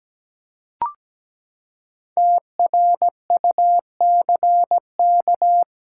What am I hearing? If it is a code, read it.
Morse code, heard as a keyed tone, about 17 wpm, 700 Hz. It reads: TRUCK
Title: Practice QSO - Exchange 3 - More Info - 17wpm